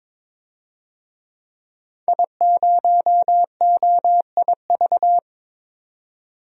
Morse code, heard as a keyed tone, about 22 wpm, 700 Hz. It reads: I0OIV
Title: DXCC-100 Call Signs - 5 Characters - 22wpm